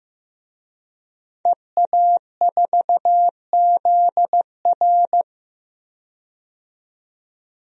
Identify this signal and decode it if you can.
Morse code, heard as a keyed tone, about 15 wpm, 700 Hz. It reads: EA4ZR
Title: DXCC-100 Call Signs - 5 Characters - 15wpm